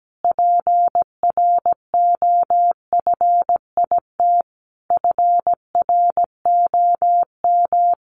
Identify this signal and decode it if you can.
Morse code, heard as a keyed tone, about 17 wpm, 700 Hz. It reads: PROFIT FROM
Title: Sentences from Top 800 Words 17wpm